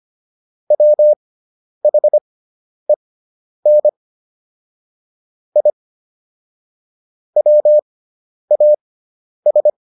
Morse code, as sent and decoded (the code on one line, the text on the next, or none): .-- .... -. / .. / .-- .- ...
WHEN I WAS